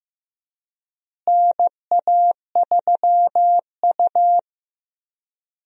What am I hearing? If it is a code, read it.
Morse code, heard as a keyed tone, about 15 wpm, 700 Hz. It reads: NA3U
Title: USA Call Signs - 4 Characters - Repeated 3x - 15wpm